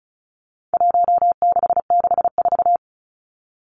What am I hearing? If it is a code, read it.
Morse code, heard as a keyed tone, about 34 wpm, 700 Hz. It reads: 1664
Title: Numbers - Contest Style - 4-Digit 35wpm